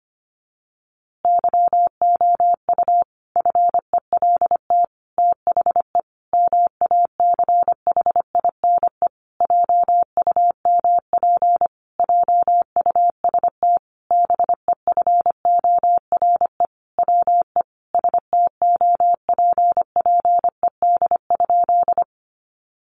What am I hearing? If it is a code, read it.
Morse code, heard as a keyed tone, about 25 wpm, 700 Hz. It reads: YOU FELT THE MACHINE JUMP JUST BEFORE WE STOPPED?
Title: A Sound Of Thunder 25wpm